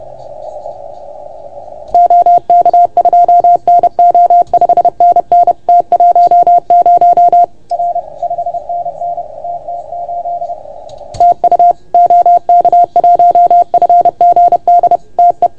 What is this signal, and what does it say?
Morse code, heard as a keyed tone, about 30 wpm, 690 Hz. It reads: OK2NO 5NN T10 R 5NN TT8 TU OK1FGD TE
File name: Archiv / CUC381 / OK1FGD CUC 381